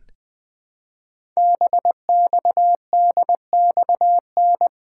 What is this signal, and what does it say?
Morse code, heard as a keyed tone, about 20 wpm, 700 Hz. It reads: BXDXN
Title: Code Groups - 5 Letter:Numbers - 20wpm